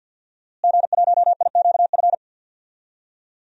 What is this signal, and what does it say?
Morse code, heard as a keyed tone, about 53 wpm, 700 Hz. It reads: G1IQF
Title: DXCC-100 Call Signs - 5 Characters - Repeated 2x - 50wpm